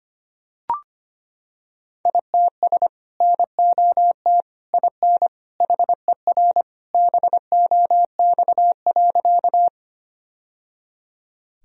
Morse code, heard as a keyed tone, about 25 wpm, 700 Hz. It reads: ITS NOT IN HER BOX.
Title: Sentences from Top 1000 Words - Review Part 2 - 25wpm